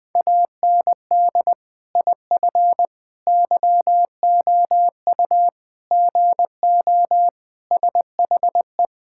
Morse code, heard as a keyed tone, about 20 wpm, 700 Hz. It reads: AND IF YOU GO SHE
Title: Sentences with 3-Word Phrases - Spoken Beginning - 20wpm